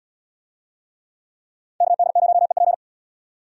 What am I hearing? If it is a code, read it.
Morse code, heard as a keyed tone, about 77 wpm, 700 Hz. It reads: KD0EP